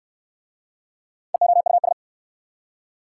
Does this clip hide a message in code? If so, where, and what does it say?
Yes: Morse code from 1.3 to 1.9 s, heard as a keyed tone, about 60 wpm, 700 Hz. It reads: EOUS